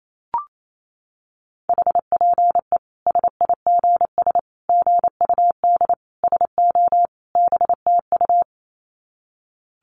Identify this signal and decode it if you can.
Morse code, heard as a keyed tone, about 28 wpm, 700 Hz. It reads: HPE SIGS GUD SO BTU
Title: Common QSO Phrases 28wpm